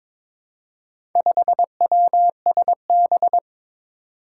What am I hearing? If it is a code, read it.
Morse code, heard as a keyed tone, about 22 wpm, 700 Hz. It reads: 5WSB
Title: Four Challenging Letter-Number Combinations 22wpm